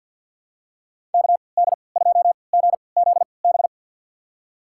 Morse code, heard as a keyed tone, about 52 wpm, spaced as for 29 wpm, 700 Hz. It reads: KD2GZB
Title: USA Call Signs - 6 Characters - 3x Character Spacing - 50wpm